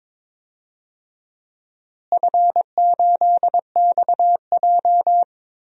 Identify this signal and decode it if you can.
Morse code, heard as a keyed tone, about 22 wpm, 700 Hz. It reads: F8XJ
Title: DXCC-100 Call Signs - 4 Characters - Repeated 2x - 22wpm